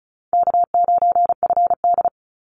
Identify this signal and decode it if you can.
Morse code, heard as a keyed tone, about 34 wpm, 700 Hz. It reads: K9FD
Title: Easy Call Signs Repeated 2x - 35wpm